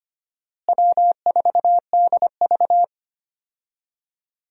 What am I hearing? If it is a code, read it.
Morse code, heard as a keyed tone, about 25 wpm, 700 Hz. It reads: W4DV
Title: USA Call Signs - 4 Characters - Repeated 2x - 25wpm